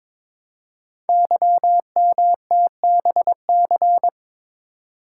Morse code, heard as a keyed tone, about 22 wpm, 700 Hz. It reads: YMTBC
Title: Code Groups - 5 Letters - 22wpm